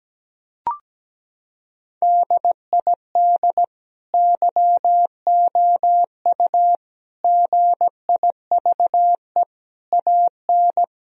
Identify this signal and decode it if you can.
Morse code, heard as a keyed tone, about 17 wpm, 700 Hz. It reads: DID YOU GIVE AN
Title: Sentences from Top 1000 Words - Review Part 3 - 17wpm